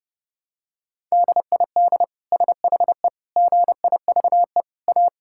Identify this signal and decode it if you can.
Morse code, heard as a keyed tone, about 30 wpm, 700 Hz. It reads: DID SHE GIVE A
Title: Sentences from Top 1000 Words - Review Part 6 - 30wpm